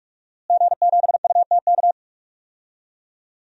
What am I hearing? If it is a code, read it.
Morse code, heard as a keyed tone, about 42 wpm, 700 Hz. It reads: G7UTK
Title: DXCC-100 Call Signs - 5 Characters - Repeated 2x - 45wpm